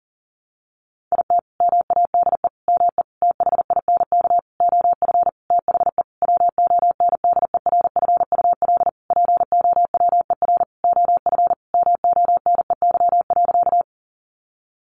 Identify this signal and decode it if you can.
Morse code, heard as a keyed tone, about 40 wpm, 700 Hz. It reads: IT MADE ME THINK OF THE WONDERFUL POWER OF MONEY.
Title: Sentences from Top 1100 Words 40wpm